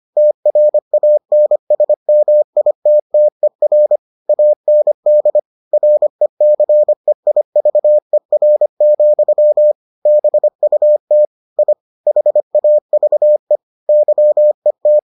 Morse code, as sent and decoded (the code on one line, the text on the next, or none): - .-. .- -. ... -- .. - - . .-. / .- -. -.. / .-. -.-. .. ...- . .-. --..-- / -... ..- - / .. / .... .- ...- . / -.-- . -
TRANSMITTER AND RECEIVER, BUT I HAVE YET